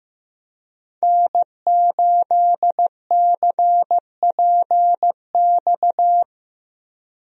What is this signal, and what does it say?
Morse code, heard as a keyed tone, about 15 wpm, 700 Hz. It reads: N8CPX
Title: USA Call Signs - 5 Characters - Repeated 3x - 15wpm